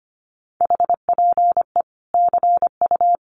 Morse code, heard as a keyed tone, about 25 wpm, 700 Hz. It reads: HPE CU
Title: Common QSO Phrases 25wpm